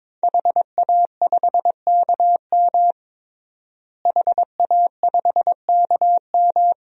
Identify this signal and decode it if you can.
Morse code, heard as a keyed tone, about 22 wpm, 700 Hz. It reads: HA5KM HA5KM
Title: DXCC-100 Call Signs - 5 Characters - Repeated 2x - 22wpm